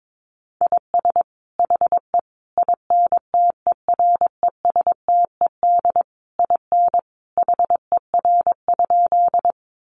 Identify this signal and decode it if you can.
Morse code, heard as a keyed tone, about 22 wpm, 700 Hz. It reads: IS HE INTERESTED IN HER?
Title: Sentences from Top 900 Words 22wpm